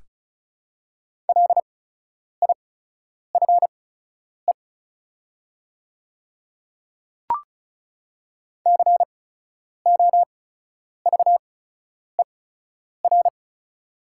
Morse code, heard as a keyed tone, about 36 wpm, spaced as for 10 wpm, 700 Hz. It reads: LIFE COVER
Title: 8x Character Spacing - Top 300 Words - 35wpm